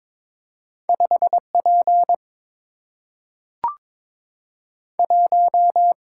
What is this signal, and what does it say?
Morse code, heard as a keyed tone, about 22 wpm, spaced as for 3 wpm, 700 Hz. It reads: 5 P1
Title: Single-Letter Single-Number 22wpm